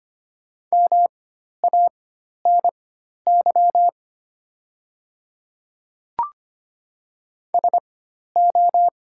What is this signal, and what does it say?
Morse code, heard as a keyed tone, about 25 wpm, spaced as for 12 wpm, 700 Hz. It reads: MANY SO
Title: 4x Character Spacing - Top 100 Words - 25wpm